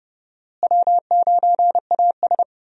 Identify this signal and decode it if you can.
Morse code, heard as a keyed tone, about 32 wpm, 700 Hz. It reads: W9AS
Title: USA Call Signs - 4 Characters - Repeated 3x - 30wpm